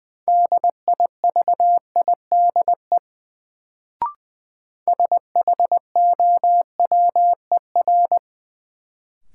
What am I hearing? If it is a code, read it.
Morse code, heard as a keyed tone, about 20 wpm, 700 Hz. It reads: DIVIDE SHOWER
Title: Sentences from Top 1600 Words - New Word Drill Alt - 20wpm